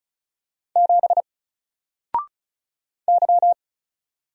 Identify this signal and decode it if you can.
Morse code, heard as a keyed tone, about 34 wpm, 700 Hz. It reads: 7 Y
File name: Single-Letter-Number-35wpm